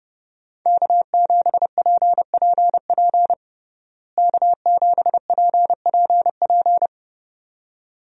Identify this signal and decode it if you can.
Morse code, heard as a keyed tone, about 32 wpm, 700 Hz. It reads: K7PPP K7PPP
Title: USA Call Signs - 5 Characters - Repeated 2x - 30wpm